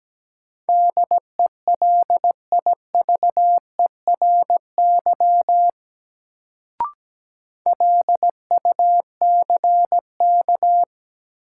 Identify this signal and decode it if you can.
Morse code, heard as a keyed tone, about 17 wpm, 700 Hz. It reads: DELIVERY LUCK
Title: Sentences from Top 1500 Words - New Word Drill - 17wpm